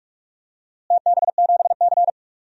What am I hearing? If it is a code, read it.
Morse code, heard as a keyed tone, about 41 wpm, 700 Hz. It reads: TB7C
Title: DXCC-100 Call Signs - 4 Characters - Repeated 3x - 45wpm